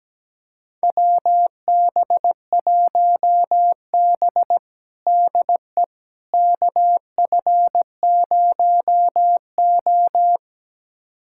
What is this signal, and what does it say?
Morse code, heard as a keyed tone, about 17 wpm, 700 Hz. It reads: WB1B DE KF0O
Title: USA Callsign DE Callsign - 4 Characters - 17wpm